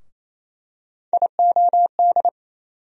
Morse code, heard as a keyed tone, about 28 wpm, 700 Hz. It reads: IOD
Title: USA Call Sign Postfixes - 3 Letters 28wpm